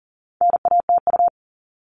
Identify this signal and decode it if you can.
Morse code, heard as a keyed tone, about 37 wpm, 700 Hz. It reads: NATU
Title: All Cut Numbers - 4-Digit 40wpm